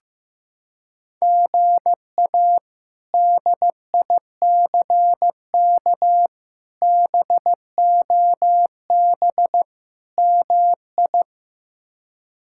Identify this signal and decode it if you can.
Morse code, heard as a keyed tone, about 15 wpm, 700 Hz. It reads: GA DICK BOB MI
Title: K1USN SST QSO Practice - 15wpm